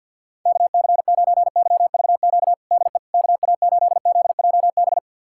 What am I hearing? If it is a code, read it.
Morse code, heard as a keyed tone, about 51 wpm, 700 Hz. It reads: KC9YVQ DE KA8ZJB
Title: USA Callsign DE Callsign - 6 Characters - 50wpm